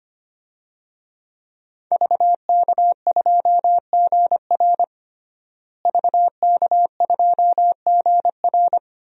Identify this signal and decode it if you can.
Morse code, heard as a keyed tone, about 25 wpm, 700 Hz. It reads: VK2GR VK2GR
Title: DXCC-100 Call Signs - 5 Characters - Repeated 3x - 25wpm